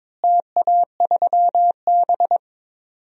Morse code, heard as a keyed tone, about 21 wpm, 700 Hz. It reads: TA3B
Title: DXCC-100 Call Signs - 4 Characters - Repeated 3x - 22wpm